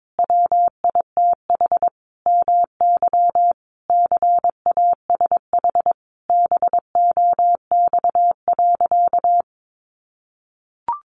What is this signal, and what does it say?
Morse code, heard as a keyed tone, about 22 wpm, 700 Hz. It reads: WITH MY CASH BOX.
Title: Sentences from Top 700 Words 22wpm